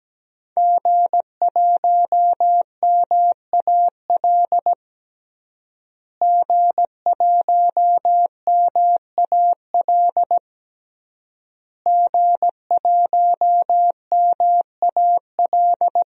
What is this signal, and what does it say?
Morse code, heard as a keyed tone, about 17 wpm, 700 Hz. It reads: G1MAL G1MAL G1MAL
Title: DXCC-100 Call Signs - 5 Characters - Repeated 3x - 17wpm